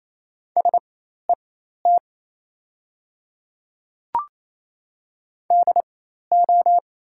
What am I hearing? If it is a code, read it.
Morse code, heard as a keyed tone, about 27 wpm, spaced as for 13 wpm, 690 Hz. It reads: SET DO
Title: 4x Character Spacing - Top 200 Words - 28wpm